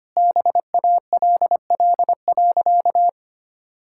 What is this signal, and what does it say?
Morse code, heard as a keyed tone, about 25 wpm, 700 Hz. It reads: BALL.
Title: Sentences from Top 1000 Words - Review Part 6 - 25wpm